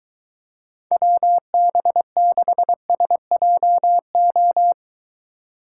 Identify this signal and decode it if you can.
Morse code, heard as a keyed tone, about 22 wpm, 700 Hz. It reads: WB6SJO